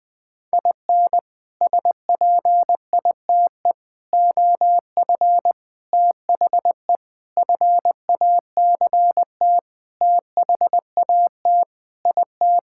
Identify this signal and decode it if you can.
Morse code, heard as a keyed tone, about 20 wpm, 700 Hz. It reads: IN SPITE OF THE FACT THAT IT
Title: Sentences with 6-Word Phrases - 20wpm